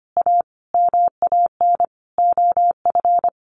Morse code, heard as a keyed tone, about 25 wpm, 700 Hz. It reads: A MAN OF
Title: Sentences from Top 1300 Words 25wpm